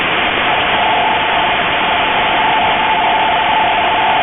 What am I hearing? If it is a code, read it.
NAVTEX (SITOR-B), about 100 baud. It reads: N SECTOR OF THE AREA WAVES
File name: Radio / SDR Audio / gqrx